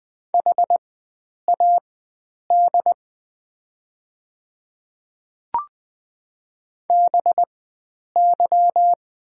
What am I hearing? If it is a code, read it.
Morse code, heard as a keyed tone, about 20 wpm, spaced as for 9 wpm, 700 Hz. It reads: HAD BY
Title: 4x Character Spacing - Top 100 Words - 20wpm